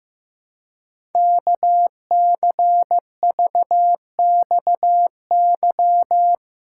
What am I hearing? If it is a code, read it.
Morse code, heard as a keyed tone, about 15 wpm, 700 Hz. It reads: KCVXY
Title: Code Groups - 5 Letters - 15wpm